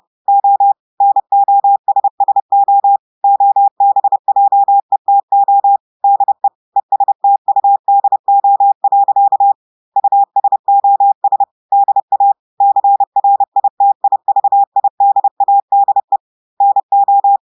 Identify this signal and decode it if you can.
Morse code, heard as a keyed tone, about 30 wpm, 800 Hz. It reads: O NOSSO OBJETO DE ESTUDO. USOS DA CRITIVIDADE NO
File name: audios / portugais / portugais texte en clair 1 vitesse 30 mots minutes